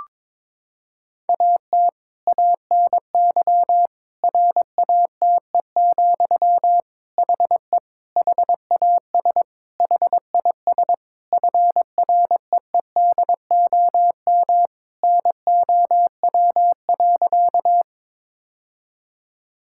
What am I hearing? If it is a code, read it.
Morse code, heard as a keyed tone, about 22 wpm, 700 Hz. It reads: AT ANY RATE, HE HAS HIS FREEDOM NOW.
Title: Sentences from Top 1000 Words - Review Part 5 - 22wpm